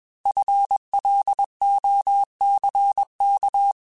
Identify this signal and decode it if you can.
Morse code, heard as a keyed tone, about 21 wpm, 800 Hz. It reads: FLOCK
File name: FL Runs / FL Runs mp3s / FL Runs @20